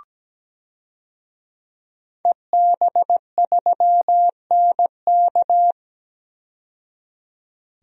Morse code, heard as a keyed tone, about 17 wpm, 700 Hz. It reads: EB3NK
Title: DXCC-100 Call Signs - 5 Characters - 17wpm